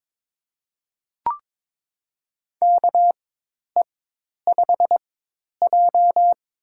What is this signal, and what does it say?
Morse code, heard as a keyed tone, about 22 wpm, spaced as for 10 wpm, 700 Hz. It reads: KE5J
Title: USA Call Signs - 4x Character Spacing - 22wpm